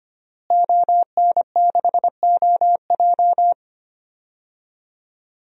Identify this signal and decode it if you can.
Morse code, heard as a keyed tone, about 25 wpm, 700 Hz. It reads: ON6OJ
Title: DXCC-100 Call Signs - 5 Characters - Repeated 3x - 25wpm